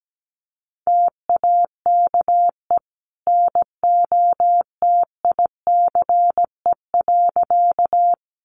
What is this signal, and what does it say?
Morse code, heard as a keyed tone, about 17 wpm, 700 Hz. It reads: TAKE NOTICE.
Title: Sentences from Top 400 Words 17wpm